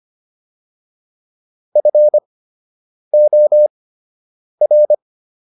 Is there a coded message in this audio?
Yes: Morse code from 1.8 to 4.9 s, heard as a keyed tone, about 25 wpm, spaced as for 8 wpm, 600 Hz. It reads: FOR